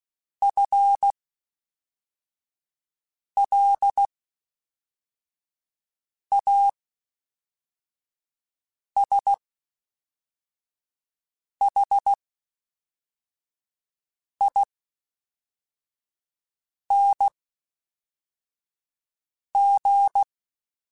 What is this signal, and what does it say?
Morse code, heard as a keyed tone, about 16 wpm, spaced as for 4 wpm, 800 Hz. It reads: FLASHING
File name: Spaced / FL Runs sp mp3s / FL Runs sp @5